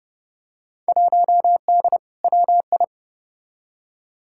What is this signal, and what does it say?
Morse code, heard as a keyed tone, about 30 wpm, 700 Hz. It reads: 1D WI
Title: Field Day Exchanges - 30wpm